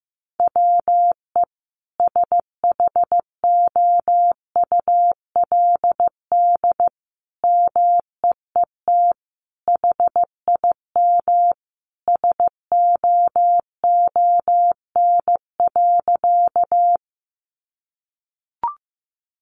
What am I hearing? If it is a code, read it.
Morse code, heard as a keyed tone, about 15 wpm, 700 Hz. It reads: WE SHOULD MEET HIM SOON.
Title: Sentences from Top 1100 Words 15wpm